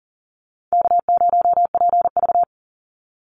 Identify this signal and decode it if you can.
Morse code, heard as a keyed tone, about 40 wpm, 700 Hz. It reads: K0PV
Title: Easy Call Signs - 40wpm